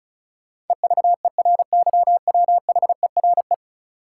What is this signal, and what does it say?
Morse code, heard as a keyed tone, about 36 wpm, 700 Hz. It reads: EVERYWHERE
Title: Sentences from Top 1600 Words - New Word Drill - 35wpm